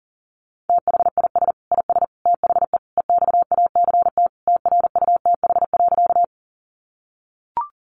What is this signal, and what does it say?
Morse code, heard as a keyed tone, about 40 wpm, 700 Hz. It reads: THIS IS THE EXACT TRUTH.
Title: Sentences from Top 1300 Words 40wpm